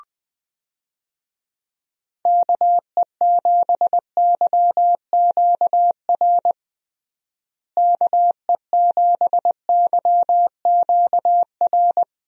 Morse code, heard as a keyed tone, about 20 wpm, 700 Hz. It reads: KE7YQR KE7YQR
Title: USA Call Signs - 6 Characters - Repeated 2x - 20wpm